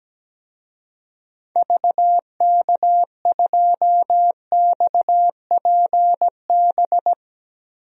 Morse code, heard as a keyed tone, about 17 wpm, 700 Hz. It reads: VK2XPB
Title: DXCC-100 Call Signs - 6 Characters - 17wpm